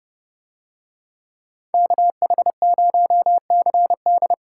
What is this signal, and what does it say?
Morse code, heard as a keyed tone, about 30 wpm, 700 Hz. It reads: KH0CD